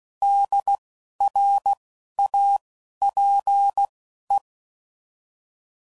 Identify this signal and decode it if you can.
Morse code, heard as a keyed tone, about 16 wpm, spaced as for 11 wpm, 800 Hz. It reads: DRAPE
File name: Prefixes / By First Letter / Ds